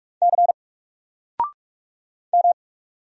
Morse code, heard as a keyed tone, about 44 wpm, 700 Hz. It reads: C M
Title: Single Character (Letter:Number:Punctuation:Prosign) 45wpm